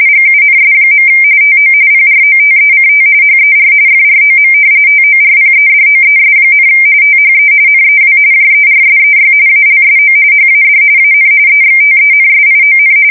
RTTY, about 45.45 baud. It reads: GUNS. TO PROTECT HER FROM TORPEDOES, THE NEW JERSEY ALSO HAD AN EXTENSION
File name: fsk170 / battleshipnj